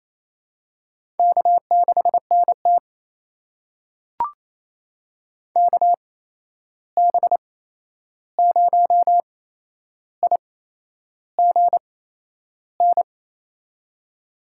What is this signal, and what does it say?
Morse code, heard as a keyed tone, about 27 wpm, spaced as for 10 wpm, 700 Hz. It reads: K6NT KB0IGN
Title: USA Call Signs - 8x Character Spacing - 28wpm